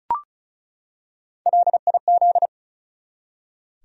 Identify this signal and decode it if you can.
Morse code, heard as a keyed tone, about 36 wpm, 700 Hz. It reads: LIZ
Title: USA Common Names - Max 4 Letters - 35wpm